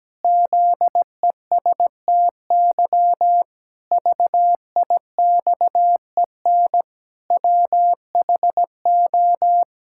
Morse code, heard as a keyed tone, about 17 wpm, 700 Hz. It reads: ZESTY VIXEN WHO
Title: Pangram Challenge v2 - 17wpm